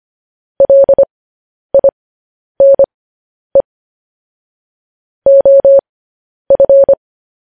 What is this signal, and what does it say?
Morse code, heard as a keyed tone, about 25 wpm, spaced as for 10 wpm, 550 Hz. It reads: LINE OF